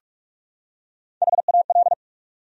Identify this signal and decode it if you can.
Morse code, heard as a keyed tone, about 46 wpm, 700 Hz. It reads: HAL
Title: USA Common Names - Max 4 Letters - 45wpm